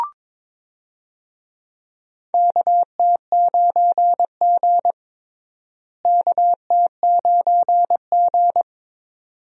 Morse code, heard as a keyed tone, about 22 wpm, 700 Hz. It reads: KT9G KT9G
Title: USA Call Signs - 4 Characters - Repeated 2x - 22wpm